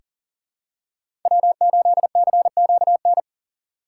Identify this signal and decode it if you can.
Morse code, heard as a keyed tone, about 40 wpm, 700 Hz. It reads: W8CQN
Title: USA Call Signs - 2x Character Spacing - 40wpm